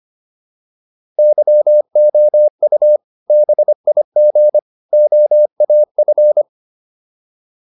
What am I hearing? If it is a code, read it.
Morse code, heard as a keyed tone, about 25 wpm, 600 Hz. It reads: YOU BIG OAF